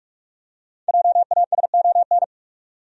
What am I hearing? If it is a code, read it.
Morse code, heard as a keyed tone, about 46 wpm, 700 Hz. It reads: JASON
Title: USA Common Names - Max 5 Letters - 45wpm